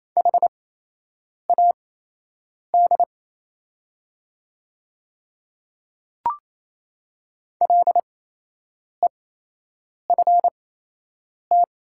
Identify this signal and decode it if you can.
Morse code, heard as a keyed tone, about 28 wpm, spaced as for 8 wpm, 700 Hz. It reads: HAD LEFT
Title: 8x Character Spacing - Top 300 Words - 28wpm